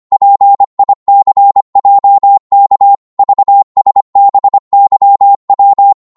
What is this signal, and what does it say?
Morse code, heard as a keyed tone, about 25 wpm, 800 Hz. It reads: PICJK VSBYW
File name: Morse / ltr-25wpm